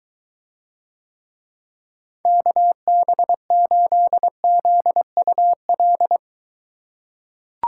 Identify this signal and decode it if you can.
Morse code, heard as a keyed tone, about 23 wpm, 700 Hz. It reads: KB8ZUL